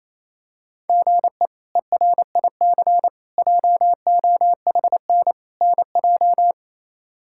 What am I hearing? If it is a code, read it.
Morse code, heard as a keyed tone, about 28 wpm, 700 Hz. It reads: GE ERIC JOHN NJ
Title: K1USN SST QSO Practice - 28wpm